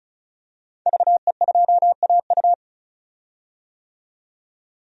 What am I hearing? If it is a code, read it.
Morse code, heard as a keyed tone, about 36 wpm, 700 Hz. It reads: VE2AU
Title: DXCC-100 Call Signs - 5 Characters - 35wpm